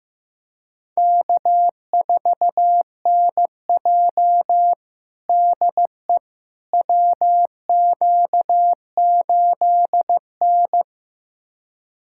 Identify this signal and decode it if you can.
Morse code, heard as a keyed tone, about 15 wpm, 700 Hz. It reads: K4NJ DE WQ8N
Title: USA Callsign DE Callsign - 4 Characters - 15wpm